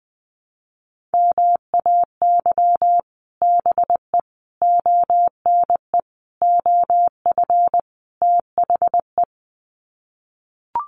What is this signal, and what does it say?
Morse code, heard as a keyed tone, about 19 wpm, 700 Hz. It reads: MAY BE ONE OF THE
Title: Sets of 5 Words Encore2 20wpm